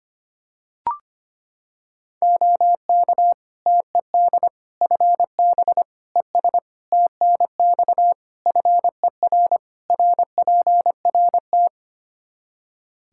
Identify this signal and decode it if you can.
Morse code, heard as a keyed tone, about 25 wpm, 700 Hz. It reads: OK TED FB ES TNX FER RPRT
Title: Practice QSO - Exchange 2 - Info - 25wpm